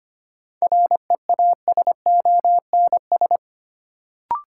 Morse code, heard as a keyed tone, about 25 wpm, 700 Hz. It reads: REASONS
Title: Sets of 4 Words - New Word Drill - 25wpm